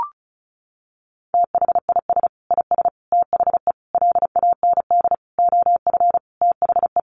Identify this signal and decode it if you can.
Morse code, heard as a keyed tone, about 34 wpm, 700 Hz. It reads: THIS IS THE LAND OF THE
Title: Sentences from Top 1300 Words 35wpm